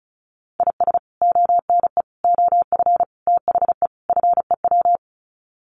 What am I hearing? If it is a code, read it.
Morse code, heard as a keyed tone, about 34 wpm, 700 Hz. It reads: IS ONE OF THE FEW
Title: Sets of 5 Words 35wpm